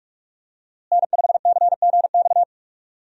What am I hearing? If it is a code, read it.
Morse code, heard as a keyed tone, about 46 wpm, 700 Hz. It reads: N5CGX
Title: USA Call Signs - 5 Characters - Repeated 3x - 45wpm